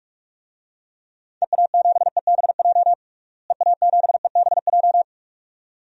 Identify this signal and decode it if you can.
Morse code, heard as a keyed tone, about 46 wpm, 700 Hz. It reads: EA7EBJ EA7EBJ
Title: DXCC-100 Call Signs - 6 Characters - Repeated 2x - 45wpm